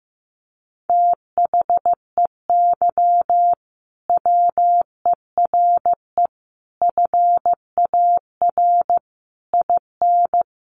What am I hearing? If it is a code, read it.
Morse code, heard as a keyed tone, about 15 wpm, 700 Hz. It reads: THEY WERE FAR IN
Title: Sentences from Top 1400 Words 15wpm